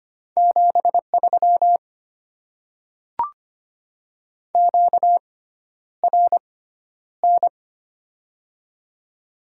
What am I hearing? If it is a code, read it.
Morse code, heard as a keyed tone, about 25 wpm, spaced as for 11 wpm, 690 Hz. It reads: 73 QRN
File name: Top 90 QSO Elements - 6x Character Spacing - 25wpm